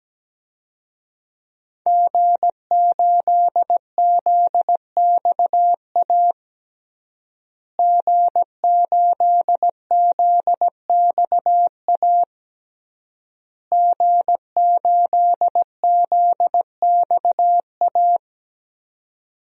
Morse code, heard as a keyed tone, about 17 wpm, 700 Hz. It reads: G8ZXA G8ZXA G8ZXA
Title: DXCC-100 Call Signs - 5 Characters - Repeated 3x - 17wpm